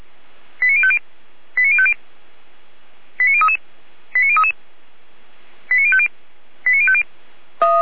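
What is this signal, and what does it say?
Five-tone selective call (ZVEI), heard as quick runs of five tones, tones of about 70 ms. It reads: 89050 89050 89033 89033 89050 89050
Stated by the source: ZVEI five-tone call received by SDR, archived by Feuerwehr Orlamünde (fire brigade).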